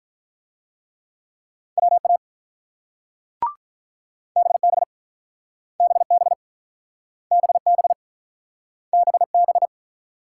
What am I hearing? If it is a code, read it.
Morse code, heard as a keyed tone, about 43 wpm, 700 Hz. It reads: WA BB BB BB BB
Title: Speed-Racing Multi-letter Phonemes - 35wpm